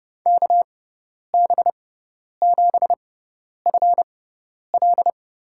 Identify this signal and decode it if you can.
Morse code, heard as a keyed tone, about 30 wpm, spaced as for 10 wpm, 700 Hz. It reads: KB7FL
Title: USA Call Signs - 6 Characters - 6x Character Spacing - 30wpm